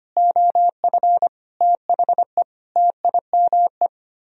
Morse code, heard as a keyed tone, about 30 wpm, 700 Hz. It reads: OF THE TIME
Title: Speed-Racing - Sets of 4 Words 25wpm